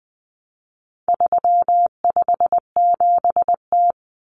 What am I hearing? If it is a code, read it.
Morse code, heard as a keyed tone, about 20 wpm, 700 Hz. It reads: 357T
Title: Common Cut Numbers - 4-Digit 20wpm